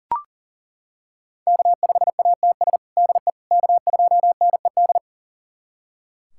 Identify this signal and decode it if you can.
Morse code, heard as a keyed tone, about 37 wpm, 700 Hz. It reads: K5ATS DE K2NED
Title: USA Callsign DE Callsign - 5 Characters - 40wpm